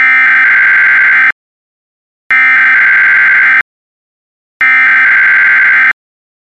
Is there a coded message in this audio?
Yes: Emergency Alert System (SAME) tones (header bursts) from 0.0 to 5.9 s, heard as harsh buzzing bursts.